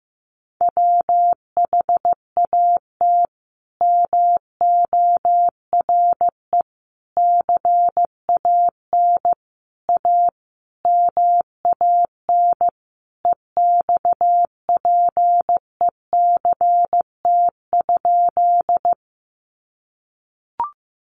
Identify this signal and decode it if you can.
Morse code, heard as a keyed tone, about 14 wpm, 700 Hz. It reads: WHAT MORE CAN A MAN EXPECT?
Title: Sentences from Top 1100 Words 15wpm